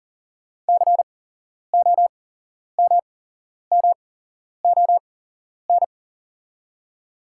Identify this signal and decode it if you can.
Morse code, heard as a keyed tone, about 40 wpm, spaced as for 11 wpm, 690 Hz. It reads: COMMON
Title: 8x Character Spacing - Top 500 Words - 40wpm